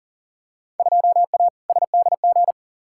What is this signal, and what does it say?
Morse code, heard as a keyed tone, about 40 wpm, 700 Hz. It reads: 2A SDG
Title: Field Day Exchanges Encore - 40wpm